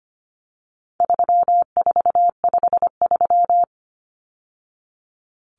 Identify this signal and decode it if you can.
Morse code, heard as a keyed tone, about 25 wpm, 700 Hz. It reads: 3453
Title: Numbers - 4-Digit 25wpm